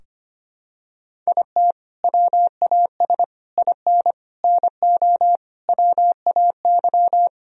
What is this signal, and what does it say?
Morse code, heard as a keyed tone, about 25 wpm, 700 Hz. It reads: IT WAS IN NO WAY
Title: Sentences from Top 1000 Words - Review Part 4 - 25wpm